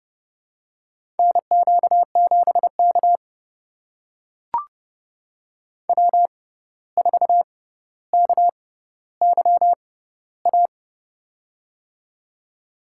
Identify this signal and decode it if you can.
Morse code, heard as a keyed tone, about 31 wpm, spaced as for 14 wpm, 700 Hz. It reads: NQ7K W4KYA
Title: USA Call Signs - 6x Character Spacing - 30wpm